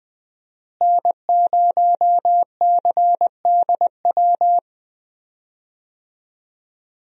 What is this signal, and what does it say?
Morse code, heard as a keyed tone, about 20 wpm, 700 Hz. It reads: N0CDW
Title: USA Call Signs - 5 Characters - Repeated 2x - 20wpm